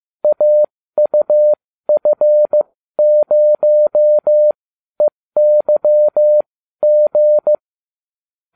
Morse code, heard as a keyed tone, about 15 wpm, spaced as for 12 wpm, 600 Hz. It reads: AUF0EYG